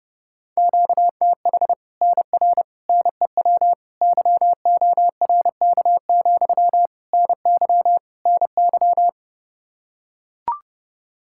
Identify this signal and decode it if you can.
Morse code, heard as a keyed tone, about 30 wpm, 700 Hz. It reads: QTH NR NEW YORK, NY NY
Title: Practice QSO - Exchange 1 - Beginning - 30wpm